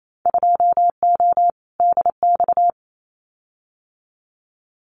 Morse code, heard as a keyed tone, about 28 wpm, 700 Hz. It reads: 2O DX
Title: Winter Field Day Exchanges Encore 28wpm